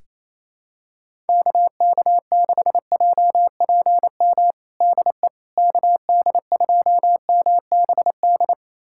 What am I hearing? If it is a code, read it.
Morse code, heard as a keyed tone, about 28 wpm, 700 Hz. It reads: KK6JPM DE KD2MBD
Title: USA Callsign DE Callsign - 6 Characters - 28wpm